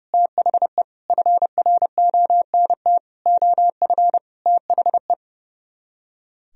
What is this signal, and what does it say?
Morse code, heard as a keyed tone, about 38 wpm, 700 Hz. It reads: THE FRONT OF THE